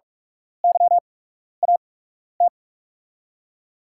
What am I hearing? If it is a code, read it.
Morse code, heard as a keyed tone, about 44 wpm, spaced as for 12 wpm, 700 Hz. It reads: YAT
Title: USA Call Signs - 5 Characters - 8x Character Spacing - 45wpm